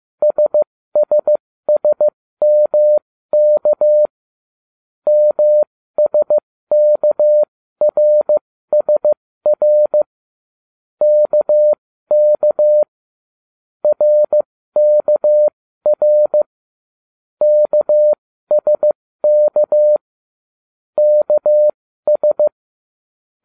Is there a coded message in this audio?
Yes: Morse code from 0.2 to 22.5 s, heard as a keyed tone, about 15 wpm, spaced as for 13 wpm, 600 Hz. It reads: SSSMK MSKRSR KK RKR KSK KS